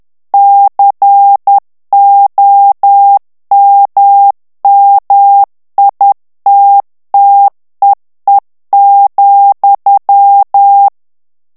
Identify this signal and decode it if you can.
Morse code, heard as a keyed tone, about 10 wpm, 800 Hz. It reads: COMMITTEE,
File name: ANGLAIS texte clair 3 vitesse 10 mots minute